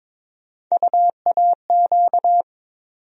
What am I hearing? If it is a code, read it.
Morse code, heard as a keyed tone, about 22 wpm, 700 Hz. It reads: UAQ
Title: USA Call Sign Postfixes - 3 Letters 22wpm